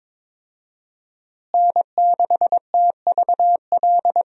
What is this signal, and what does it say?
Morse code, heard as a keyed tone, about 22 wpm, 700 Hz. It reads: N6TVL
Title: USA Call Signs - 5 Characters - Repeated 2x - 22wpm